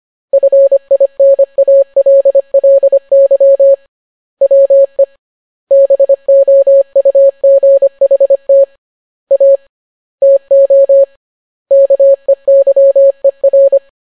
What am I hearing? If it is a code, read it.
Morse code, heard as a keyed tone, about 25 wpm, 550 Hz. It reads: FINALLY WE BOUGHT A TO KEYER